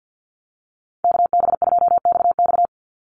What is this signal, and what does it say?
Morse code, heard as a keyed tone, about 49 wpm, 700 Hz. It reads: KB2XX
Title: NA SOTA Chaser Callsigns 50wpm